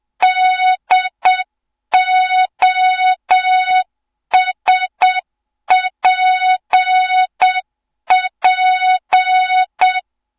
Morse code, heard as a keyed tone, about 7 wpm, 3700 Hz. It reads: DOSPP
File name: Morze-HA2MN / hang / 07 GYAKORLAT